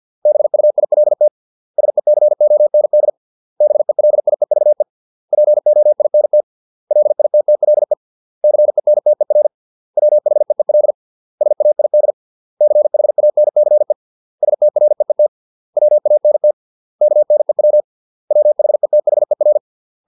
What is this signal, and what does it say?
Morse code, heard as a keyed tone, about 50 wpm, 600 Hz. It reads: BUILT SECOND BELIEVE POINT LITTLE CENTER WHEEL SAID CHANCE STREET WANT KNEW WHETHER